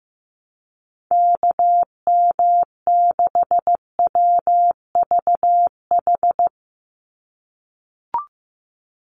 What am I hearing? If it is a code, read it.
Morse code, heard as a keyed tone, about 15 wpm, 700 Hz. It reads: KM6WVH
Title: USA Call Signs 15wpm